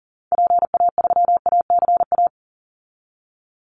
Morse code, heard as a keyed tone, about 40 wpm, 700 Hz. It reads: PA3ACA
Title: Call Signs Repeated 3x - 40wpm